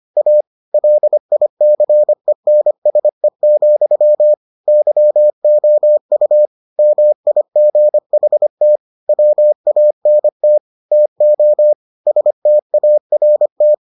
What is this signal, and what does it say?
Morse code, heard as a keyed tone, about 25 wpm, 600 Hz. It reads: A LICENSE, YOU MIGHT WANT TO START